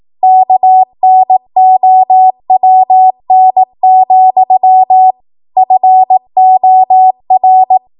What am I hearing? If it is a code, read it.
Morse code, heard as a keyed tone, about 18 wpm, 750 Hz. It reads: KNOWN, FOR